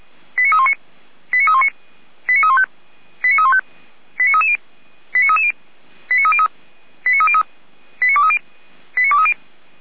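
Five-tone selective call (ZVEI), heard as quick runs of five tones, tones of about 70 ms. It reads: 89319 89319 89315 89315 89330 89330 89303 89303 89230 89230
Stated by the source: ZVEI five-tone call received by SDR.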